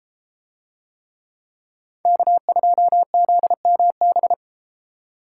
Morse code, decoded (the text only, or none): K2ZMB